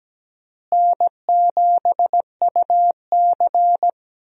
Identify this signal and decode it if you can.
Morse code, heard as a keyed tone, about 17 wpm, 700 Hz. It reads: N7UC
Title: USA Call Signs - 4 Characters - Repeated 2x - 17wpm